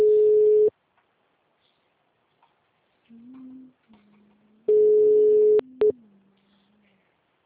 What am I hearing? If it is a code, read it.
Ringback tone (Europe, 425 Hz).